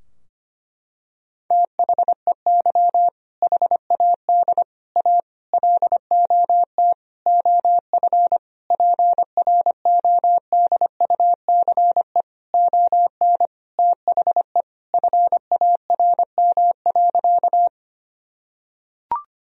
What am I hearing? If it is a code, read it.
Morse code, heard as a keyed tone, about 25 wpm, 700 Hz. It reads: THEY HAD A LOT OF PRODUCE ON THE FARM.
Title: Sentences from Top 1000 Words - Review Part 3 - 25wpm